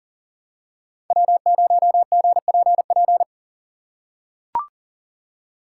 Morse code, heard as a keyed tone, about 40 wpm, 700 Hz. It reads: W0GPP
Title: USA Call Signs - 5 Characters - 2x Character Spacing - 40wpm